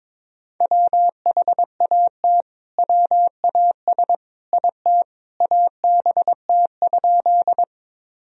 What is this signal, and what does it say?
Morse code, heard as a keyed tone, about 22 wpm, 700 Hz. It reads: WHAT WAS IT ABT?
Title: Sentences from Top 100 Words and Abbreviations 22wpm